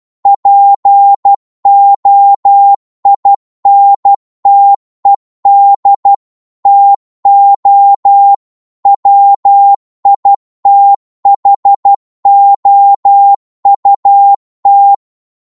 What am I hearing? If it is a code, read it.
Morse code, heard as a keyed tone, about 12 wpm, 800 Hz. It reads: POINTED TO WITHOUT